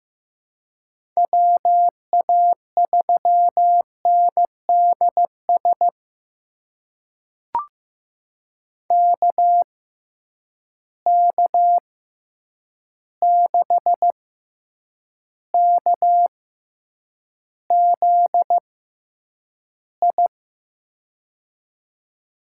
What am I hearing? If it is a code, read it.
Morse code, heard as a keyed tone, about 15 wpm, spaced as for 8 wpm, 700 Hz. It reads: WA3NDS KK6KZI
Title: USA Call Signs - 6x Character Spacing - 15wpm